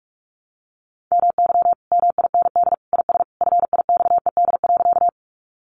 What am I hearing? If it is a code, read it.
Morse code, heard as a keyed tone, about 44 wpm, 700 Hz. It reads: MY MIND IS FIXED.
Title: Sentences from Top 1100 Words 45wpm